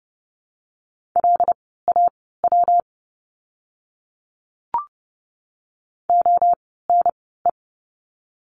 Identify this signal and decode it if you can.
Morse code, heard as a keyed tone, about 30 wpm, spaced as for 17 wpm, 700 Hz. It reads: LAW ONE
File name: course / 30:10wpm / 115 Lesson 19f - Letter w - Words 30wpm